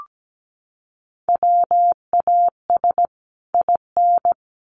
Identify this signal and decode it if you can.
Morse code, heard as a keyed tone, about 17 wpm, 700 Hz. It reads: WAS IN
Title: Sets of 2 Words 17wpm